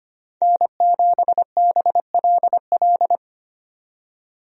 Morse code, heard as a keyed tone, about 25 wpm, 700 Hz. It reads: N7BLL
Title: USA Call Signs - 5 Characters - 25wpm